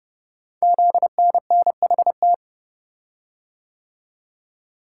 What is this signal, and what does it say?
Morse code, heard as a keyed tone, about 30 wpm, 700 Hz. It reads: ZNNHT
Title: Code Groups - 5 Letters - 30wpm